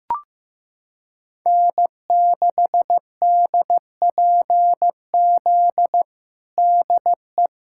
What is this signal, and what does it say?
Morse code, heard as a keyed tone, about 15 wpm, 700 Hz. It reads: N6DPZ DE
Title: USA Callsign DE Callsign - 5 Characters - 15wpm